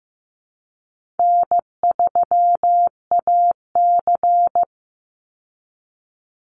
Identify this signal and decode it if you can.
Morse code, heard as a keyed tone, about 15 wpm, 700 Hz. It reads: N3AC
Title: Call Signs Repeated 2x - 15wpm